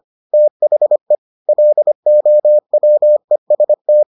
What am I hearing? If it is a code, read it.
Morse code, heard as a keyed tone, about 25 wpm, 600 Hz. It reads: THE LOWEST